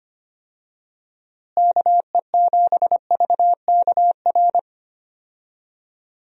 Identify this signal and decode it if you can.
Morse code, heard as a keyed tone, about 25 wpm, 700 Hz. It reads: KE7VKR
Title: USA Call Signs - 6 Characters - Repeated 3x - 25wpm